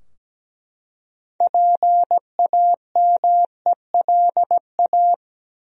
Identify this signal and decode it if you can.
Morse code, heard as a keyed tone, about 17 wpm, 700 Hz. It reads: PAMELA
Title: USA Common Names - 17wpm